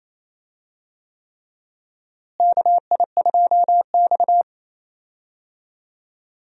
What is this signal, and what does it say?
Morse code, heard as a keyed tone, about 28 wpm, 700 Hz. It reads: KI2X
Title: USA Call Signs - 4 Characters - Repeated 3x - 28wpm